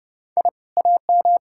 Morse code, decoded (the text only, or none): I AM